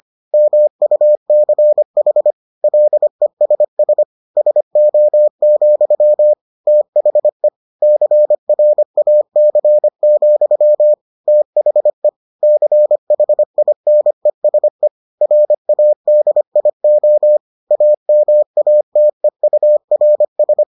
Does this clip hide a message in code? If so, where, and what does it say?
Yes: Morse code from 0.3 to 20.6 s, heard as a keyed tone, about 25 wpm, 600 Hz. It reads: MUCH LESS SO, THE CRAC, THE CHINESE RADIO AMATEURS